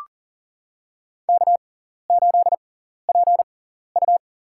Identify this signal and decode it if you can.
Morse code, heard as a keyed tone, about 40 wpm, spaced as for 14 wpm, 700 Hz. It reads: K8PU
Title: USA Call Signs - 4 Characters - 6x Character Spacing - 40wpm